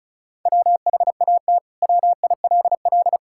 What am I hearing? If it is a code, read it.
Morse code, heard as a keyed tone, about 36 wpm, 700 Hz. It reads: WHAT WILL
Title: Sentences from Top 1000 Words - Review Part 5 - 35wpm